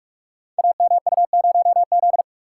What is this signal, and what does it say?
Morse code, heard as a keyed tone, about 44 wpm, 700 Hz. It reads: AMU0Z
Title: Code Groups - 5 Letter:Numbers - 45wpm